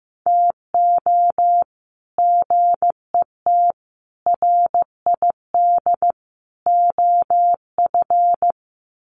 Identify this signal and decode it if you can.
Morse code, heard as a keyed tone, about 15 wpm, 700 Hz. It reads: TO GET RID OF
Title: Sets of 4 Words 15wpm